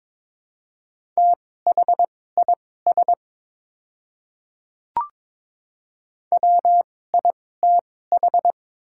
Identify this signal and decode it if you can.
Morse code, heard as a keyed tone, about 21 wpm, spaced as for 16 wpm, 700 Hz. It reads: THIS WITH
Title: 2x Character Spacing - Top 100 Words - 22wpm